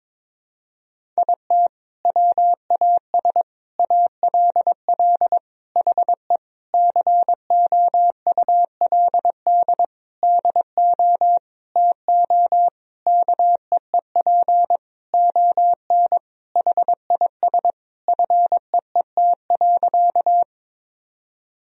Morse code, heard as a keyed tone, about 22 wpm, 700 Hz. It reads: IT WAS ALL HE COULD DO TO KEEP ON HIS FEET.
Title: Sentences from Top 1000 Words - Review Part 1 - 22wpm